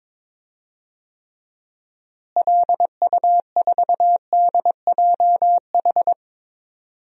Morse code, heard as a keyed tone, about 21 wpm, 700 Hz. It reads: LU4DJH
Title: DXCC-100 Call Signs - 6 Characters - Repeated 2x - 22wpm